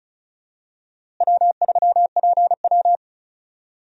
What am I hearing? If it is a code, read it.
Morse code, heard as a keyed tone, about 36 wpm, 700 Hz. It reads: W3PW
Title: USA Call Signs - 4 Characters - Repeated 3x - 35wpm